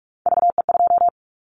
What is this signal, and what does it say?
Morse code, heard as a keyed tone, about 46 wpm, 700 Hz. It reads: VE2
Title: Callsign Prefixes for Top 25 DXCC - 45wpm